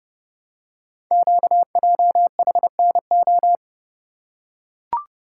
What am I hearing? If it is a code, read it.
Morse code, heard as a keyed tone, about 30 wpm, 700 Hz. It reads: QJHNO
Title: Code Groups - 5 Letters - 30wpm